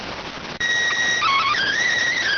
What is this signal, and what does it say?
SSTV (slow-scan television), PD 290.